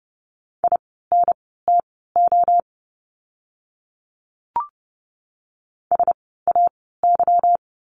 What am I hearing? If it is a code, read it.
Morse code, heard as a keyed tone, about 30 wpm, spaced as for 17 wpm, 700 Hz. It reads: INTO SAY
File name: course / 30:10wpm / 237 Lesson 38f - Letter q - Words 30wpm